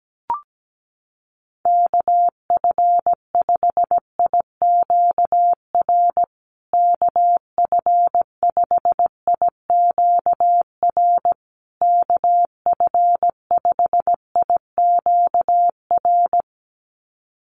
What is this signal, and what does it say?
Morse code, heard as a keyed tone, about 17 wpm, 700 Hz. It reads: KF5IQR KF5IQR KF5IQR
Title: USA Call Signs Repeated 3x 17wpm